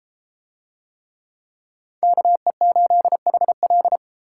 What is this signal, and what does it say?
Morse code, heard as a keyed tone, about 31 wpm, 700 Hz. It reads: KE8HL